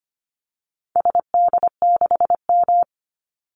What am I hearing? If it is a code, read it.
Morse code, heard as a keyed tone, about 25 wpm, 700 Hz. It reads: SD6M
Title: Call Signs - 25wpm